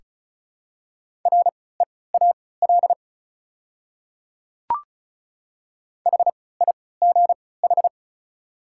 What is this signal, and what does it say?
Morse code, heard as a keyed tone, about 34 wpm, spaced as for 20 wpm, 690 Hz. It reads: REAL HIGH
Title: 3x Character Spacing - Top 500 Words - 35wpm